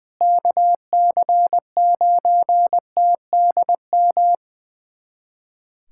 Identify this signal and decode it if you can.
Morse code, heard as a keyed tone, about 20 wpm, 700 Hz. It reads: KC9TDM
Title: USA Callsign DE Callsign - 6 Characters - 20wpm